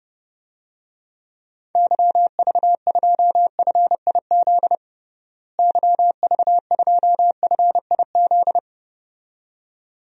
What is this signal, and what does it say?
Morse code, heard as a keyed tone, about 32 wpm, 700 Hz. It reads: YV2FIZ YV2FIZ
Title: DXCC-100 Call Signs - 6 Characters - Repeated 2x - 30wpm